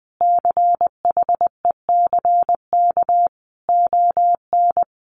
Morse code, heard as a keyed tone, about 20 wpm, 700 Hz. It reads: CHECK ON
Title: Sentences from Top 1400 Words 20wpm